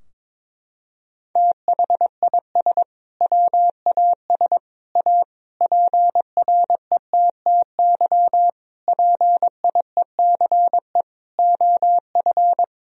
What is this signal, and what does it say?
Morse code, heard as a keyed tone, about 22 wpm, 700 Hz. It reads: THIS WAS A PRETTY PIECE OF
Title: Sentences from Top 1000 Words - Review Part 2 - 22wpm